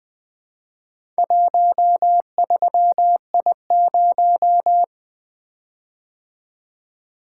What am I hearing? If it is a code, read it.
Morse code, heard as a keyed tone, about 24 wpm, 700 Hz. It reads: 13I0
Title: Speed-Racing - Four Letter-Number Combinations - 20wpm